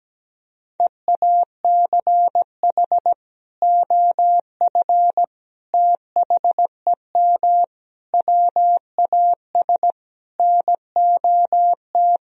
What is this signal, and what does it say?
Morse code, heard as a keyed tone, about 17 wpm, 700 Hz. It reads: EACH OF THEM WAS NOT
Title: Sentences from Top 1000 Words - Review Part 1 - 17wpm